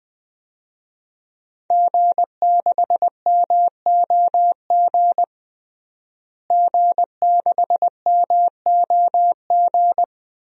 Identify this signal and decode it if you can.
Morse code, heard as a keyed tone, about 20 wpm, 700 Hz. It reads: G6MOG G6MOG
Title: DXCC-100 Call Signs - 5 Characters - Repeated 3x - 20wpm